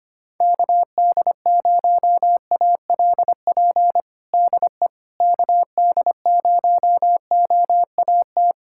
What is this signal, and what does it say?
Morse code, heard as a keyed tone, about 25 wpm, 700 Hz. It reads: KD0ALP DE KD0OAT
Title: USA Callsign DE Callsign 25wpm